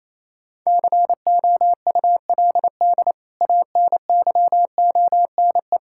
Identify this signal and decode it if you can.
Morse code, heard as a keyed tone, about 28 wpm, 700 Hz. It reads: COULD ANYONE
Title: Sentences from Top 1600 Words 28wpm